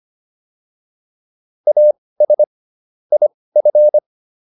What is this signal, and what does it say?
Morse code, heard as a keyed tone, about 25 wpm, spaced as for 18 wpm, 600 Hz. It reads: AS IF